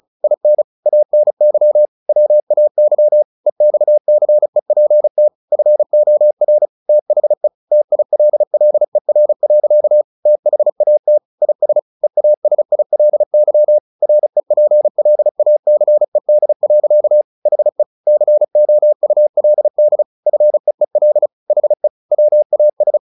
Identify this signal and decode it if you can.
Morse code, heard as a keyed tone, about 34 wpm, 600 Hz. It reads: IN ANY WAY EXCEPT FOR THE TILLER. THAT IS EASILY REPLACED. HE COULD FEEL HE WAS
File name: the old man and the sea 2 35wpm